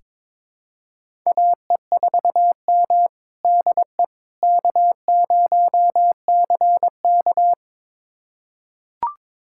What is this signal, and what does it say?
Morse code, heard as a keyed tone, about 22 wpm, 700 Hz. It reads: AE4M DE K0CK
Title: USA Callsign DE Callsign - 4 Characters - 22wpm